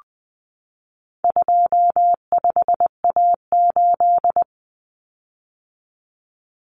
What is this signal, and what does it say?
Morse code, heard as a keyed tone, about 20 wpm, 700 Hz. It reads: 25A8
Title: Common Cut Numbers - 4-Digit 20wpm